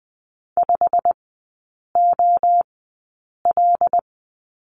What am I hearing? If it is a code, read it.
Morse code, heard as a keyed tone, about 20 wpm, spaced as for 8 wpm, 690 Hz. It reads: 5OL
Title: 087 Lesson 15e - Number 5 - Three Characters 20wpm